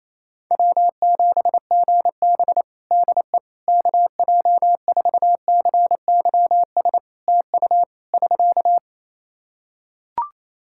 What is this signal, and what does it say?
Morse code, heard as a keyed tone, about 28 wpm, 700 Hz. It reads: W7GB DE KJ4CYS TU <SK>
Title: Practice QSO - Exchange 4 - Ending - 28wpm